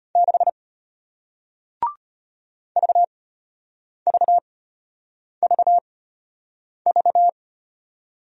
Morse code, heard as a keyed tone, about 35 wpm, spaced as for 8 wpm, 700 Hz. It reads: 6 VVVV